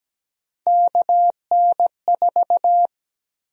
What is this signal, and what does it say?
Morse code, heard as a keyed tone, about 17 wpm, 700 Hz. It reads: KN4
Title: USA Call Sign Prefixes - Repeated 2x - 17wpm